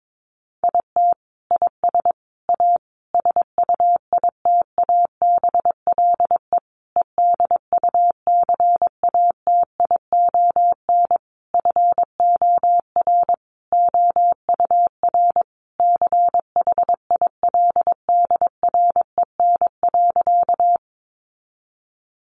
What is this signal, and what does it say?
Morse code, heard as a keyed tone, about 22 wpm, 700 Hz. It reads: IT IS A SUITABLE EDUCATION FOR OUR CHILDREN.
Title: Sentences from Top 1400 Words 22wpm